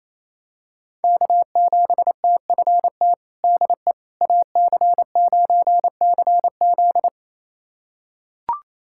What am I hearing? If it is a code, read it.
Morse code, heard as a keyed tone, about 28 wpm, 700 Hz. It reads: K7TFT DE AC9CZ
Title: USA Callsign DE Callsign - 5 Characters - 28wpm